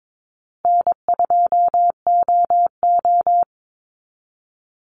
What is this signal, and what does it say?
Morse code, heard as a keyed tone, about 22 wpm, 700 Hz. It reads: N2OO
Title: Easy Call Signs Repeated 3x - 22wpm